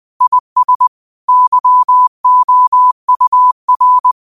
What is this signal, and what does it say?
Morse code, heard as a keyed tone, about 20 wpm, 1000 Hz. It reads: IS YOUR